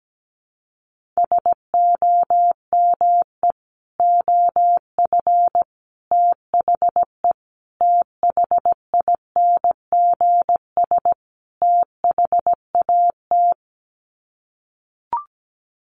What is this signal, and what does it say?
Morse code, heard as a keyed tone, about 17 wpm, 700 Hz. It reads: SOME OF THE THINGS THAT
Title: Sets of 5 Words 17wpm